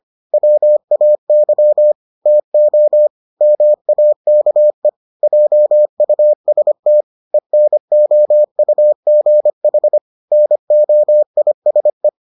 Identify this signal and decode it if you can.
Morse code, heard as a keyed tone, about 25 wpm, 600 Hz. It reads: WAY TO MAKE JUST ENOUGH NOISE